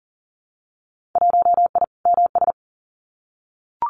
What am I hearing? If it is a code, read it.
Morse code, heard as a keyed tone, about 40 wpm, 700 Hz. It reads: 1I MS
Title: Winter Field Day Exchanges Encore 40wpm